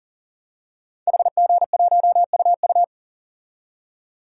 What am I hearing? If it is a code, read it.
Morse code, heard as a keyed tone, about 40 wpm, 700 Hz. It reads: HG1UU
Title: DXCC-100 Call Signs - 5 Characters - 40wpm